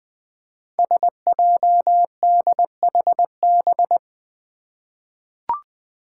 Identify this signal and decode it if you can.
Morse code, heard as a keyed tone, about 20 wpm, 700 Hz. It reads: SJDHB
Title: Five Challenging Letter-Number Combinations 20wpm